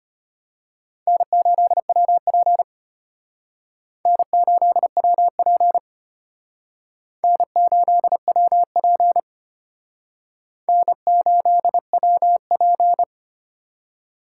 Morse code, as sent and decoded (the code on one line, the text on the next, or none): -. ---.. .-- .--. / -. ---.. .-- .--. / -. ---.. .-- .--. / -. ---.. .-- .--.
N8WP N8WP N8WP N8WP